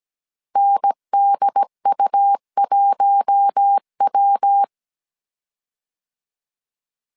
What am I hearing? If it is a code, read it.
Morse code, heard as a keyed tone, about 17 wpm, 800 Hz. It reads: NDU1W
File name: CW / DET / 1 6 1500C S A 16wpm 2635